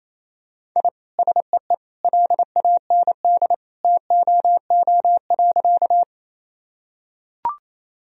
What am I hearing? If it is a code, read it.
Morse code, heard as a keyed tone, about 28 wpm, 700 Hz. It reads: I SEE LAND TOO.
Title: Sentences from Top 1000 Words - Review Part 1 - 28wpm